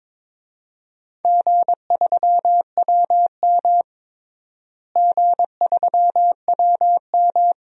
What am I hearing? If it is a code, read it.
Morse code, heard as a keyed tone, about 22 wpm, 700 Hz. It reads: G3WM G3WM
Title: DXCC-100 Call Signs - 4 Characters - Repeated 2x - 22wpm